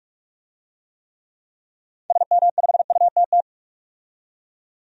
Morse code, heard as a keyed tone, about 43 wpm, 700 Hz. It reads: SM5UTT
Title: DXCC-100 Call Signs - 6 Characters - 45wpm